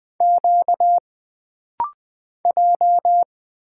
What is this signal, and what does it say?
Morse code, heard as a keyed tone, about 20 wpm, 700 Hz. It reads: Q J
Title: Single Character (Letter:Number:Punctuation:Prosign) - Rapid-Fire - 20wpm